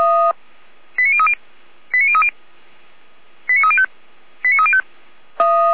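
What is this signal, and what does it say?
Five-tone selective call (ZVEI), heard as quick runs of five tones, tones of about 70 ms. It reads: 89030 89030 89305 89305
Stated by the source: Five-tone ZVEI selective call received by SDR.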